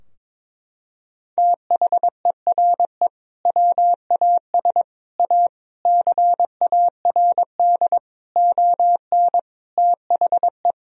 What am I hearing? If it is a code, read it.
Morse code, heard as a keyed tone, about 22 wpm, 700 Hz. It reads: THERE WAS A CARD ON THE
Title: Sentences from Top 1000 Words - Review Part 4 - 22wpm